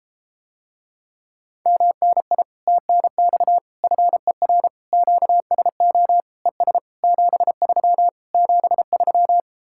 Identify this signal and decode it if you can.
Morse code, heard as a keyed tone, about 34 wpm, 700 Hz. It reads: MNI TNX FER QSO ES 73 73